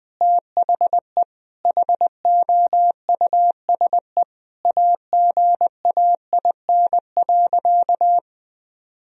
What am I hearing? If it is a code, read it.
Morse code, heard as a keyed tone, about 20 wpm, 700 Hz. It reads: THE HOUSE AGAIN.
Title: Sentences from Top 1000 Words - Review Part 3 - 20wpm